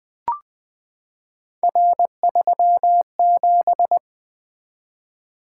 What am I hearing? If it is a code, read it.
Morse code, heard as a keyed tone, about 20 wpm, 700 Hz. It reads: R37
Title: Three Letter-Number Combinations - 20wpm